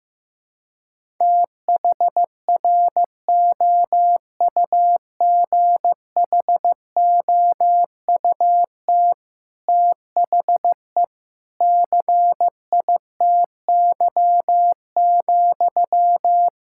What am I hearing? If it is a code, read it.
Morse code, heard as a keyed tone, about 15 wpm, 700 Hz. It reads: THROUGHOUT THE CITY,
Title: Sentences from Top 1000 Words - Review Part 5 - 15wpm